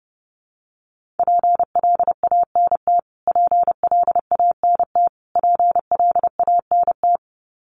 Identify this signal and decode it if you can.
Morse code, heard as a keyed tone, about 30 wpm, 700 Hz. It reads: PLANT PLANT PLANT
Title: Top 300 Words Repeated 30wpm